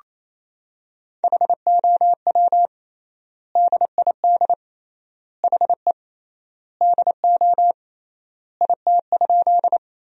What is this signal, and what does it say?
Morse code, heard as a keyed tone, about 28 wpm, 700 Hz. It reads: HOW DID HE DO IT?
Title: 3x Word-Spacing - Sentences from Top 100 Words - 28wpm